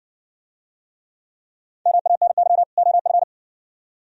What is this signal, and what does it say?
Morse code, heard as a keyed tone, about 60 wpm, 700 Hz. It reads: MANY OF